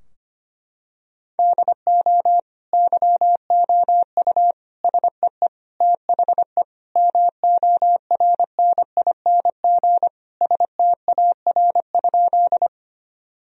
Morse code, heard as a keyed tone, about 25 wpm, 700 Hz. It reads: DO YOU SEE THE MORNING STAR?
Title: Sentences from Top 1000 Words - Review Part 2 - 25wpm